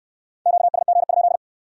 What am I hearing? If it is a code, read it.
Morse code, heard as a keyed tone, about 64 wpm, 700 Hz. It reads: QSG?